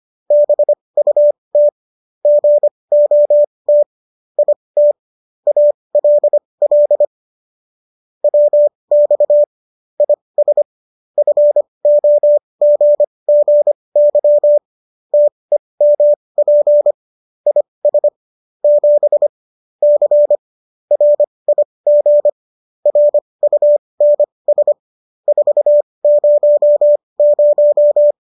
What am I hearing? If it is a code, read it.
Morse code, heard as a keyed tone, about 25 wpm, spaced as for 20 wpm, 590 Hz. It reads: BUT GOT IT ALL WX IS FOGGY TEMP IS 7 C RIG RUNS 400